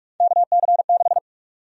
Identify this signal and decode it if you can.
Morse code, heard as a keyed tone, about 41 wpm, 700 Hz. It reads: KC6
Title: USA Call Sign Prefixes - 45wpm